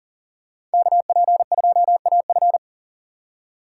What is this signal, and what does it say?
Morse code, heard as a keyed tone, about 40 wpm, 700 Hz. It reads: KP2AF
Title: USA Call Signs - 5 Characters - 4x Character Spacing - 40wpm